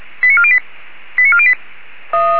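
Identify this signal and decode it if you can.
Five-tone selective call (ZVEI), heard as quick runs of five tones, tones of about 70 ms. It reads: 89408 89408
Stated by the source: ZVEI five-tone selective call received by SDR.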